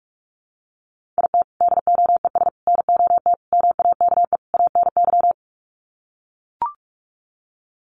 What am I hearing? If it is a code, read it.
Morse code, heard as a keyed tone, about 44 wpm, 690 Hz. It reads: IT DOES NOT MAKE ANY
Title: Sets of 5 Words Encore2 45wpm